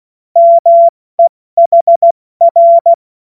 Morse code, heard as a keyed tone, about 16 wpm, spaced as for 14 wpm, 680 Hz. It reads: MEHR
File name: Morsekurs / Witze / Witz 31. August 2021, 70 F80